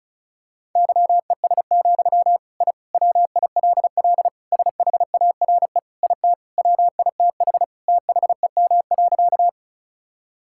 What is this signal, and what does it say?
Morse code, heard as a keyed tone, about 36 wpm, 690 Hz. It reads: YES, I WILL SHARE IT WITH THEM.
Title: Sentences from Top 1000 Words - Review Part 6 - 35wpm